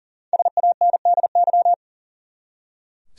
Morse code, heard as a keyed tone, about 40 wpm, 700 Hz. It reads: SANDY
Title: USA Common Names - Max 5 Letters - 40wpm